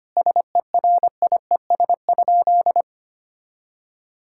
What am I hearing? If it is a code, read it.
Morse code, heard as a keyed tone, about 25 wpm, 700 Hz. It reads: SERIES?
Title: Sentences from Top 1000 Words - Review Part 5 - 25wpm